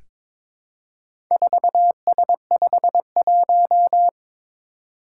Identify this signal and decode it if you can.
Morse code, heard as a keyed tone, about 22 wpm, 700 Hz. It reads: 4S51
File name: practice / 22wpm / Four Letter-Number Combinations - 22wpm